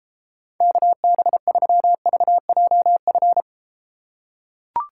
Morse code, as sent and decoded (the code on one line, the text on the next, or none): -.- -... ...-- ...- .--- ..-.
KB3VJF